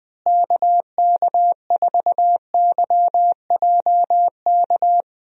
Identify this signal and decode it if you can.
Morse code, heard as a keyed tone, about 20 wpm, 700 Hz. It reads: KK4YJK
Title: USA Call Signs - 6 Characters - Repeated 2x - 20wpm